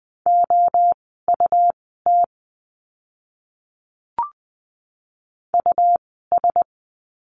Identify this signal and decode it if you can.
Morse code, heard as a keyed tone, about 20 wpm, spaced as for 14 wpm, 700 Hz. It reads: OUT US
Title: 192 Lesson 31f - Letter v - Words 20wpm